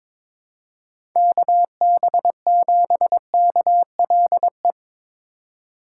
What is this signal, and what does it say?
Morse code, heard as a keyed tone, about 22 wpm, 700 Hz. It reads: KB7KLE
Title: USA Call Signs - 4x Character Spacing - 22wpm